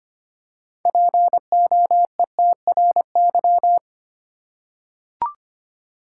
Morse code, heard as a keyed tone, about 25 wpm, 700 Hz. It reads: POETRY
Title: Six Letter Words - 25wpm